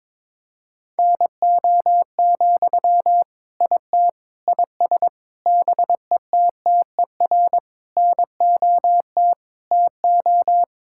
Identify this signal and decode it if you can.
Morse code, heard as a keyed tone, about 22 wpm, 700 Hz. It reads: NO, IT IS BETTER NOT TO
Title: Sentences from Top 1600 Words 22wpm